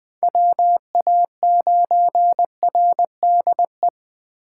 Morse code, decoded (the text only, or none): WA9RDE